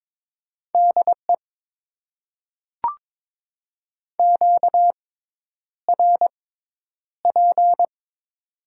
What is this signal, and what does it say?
Morse code, heard as a keyed tone, about 21 wpm, spaced as for 10 wpm, 700 Hz. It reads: DE QRP
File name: Top 90 QSO Elements - 6x Character Spacing - 22wpm